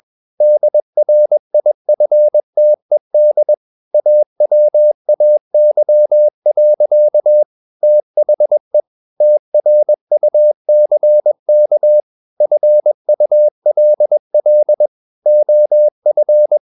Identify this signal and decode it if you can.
Morse code, heard as a keyed tone, about 21 wpm, 600 Hz. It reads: DRIFTED AWAY. THE TRUCK FULL OF
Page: CW med Gnister